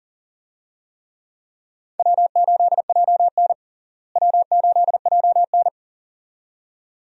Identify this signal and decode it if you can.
Morse code, heard as a keyed tone, about 40 wpm, 700 Hz. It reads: W8JN W8JN
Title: USA Call Signs - 4 Characters - Repeated 2x - 40wpm